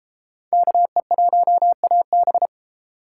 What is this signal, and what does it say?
Morse code, heard as a keyed tone, about 33 wpm, 700 Hz. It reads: KE1AB